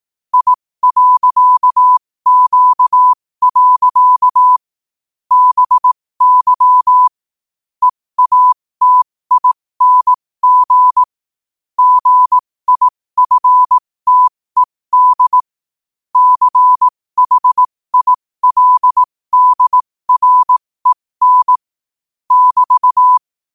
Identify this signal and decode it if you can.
Morse code, heard as a keyed tone, about 18 wpm, spaced as for 15 wpm, 1000 Hz. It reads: I.Q. BY EATING GIFTED CHILDREN <BT>
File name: qotdfiles / 15wpm